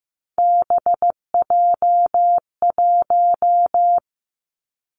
Morse code, heard as a keyed tone, about 15 wpm, 700 Hz. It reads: BJ1
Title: Three Challenging Letter-Number Combinations 15wpm